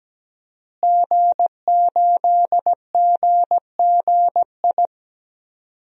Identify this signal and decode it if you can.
Morse code, heard as a keyed tone, about 17 wpm, 700 Hz. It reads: G8GGI
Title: DXCC-100 Call Signs - 5 Characters - Repeated 2x - 17wpm